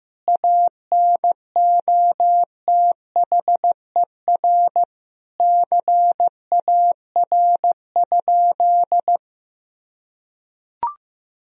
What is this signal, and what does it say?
Morse code, heard as a keyed tone, about 16 wpm, 700 Hz. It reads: ANOTHER CAR?
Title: Sentences from Top 1000 Words - Review Part 1 - 15wpm